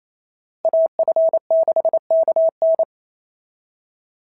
Morse code, heard as a keyed tone, about 28 wpm, 690 Hz. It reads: AF6KN
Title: SOTA QSO Practice 28wpm